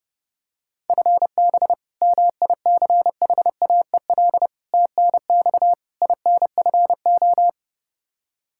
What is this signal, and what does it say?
Morse code, heard as a keyed tone, about 29 wpm, 700 Hz. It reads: FB MICHAEL TNX INFO
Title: Practice QSO - Exchange 3 - More Info - 30wpm